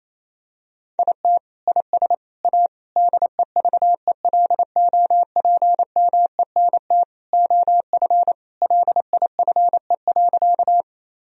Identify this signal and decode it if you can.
Morse code, heard as a keyed tone, about 28 wpm, 700 Hz. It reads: IT IS A DEVELOPMENT OF LIFE.
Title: Sentences from Top 1000 Words - Review Part 4 - 28wpm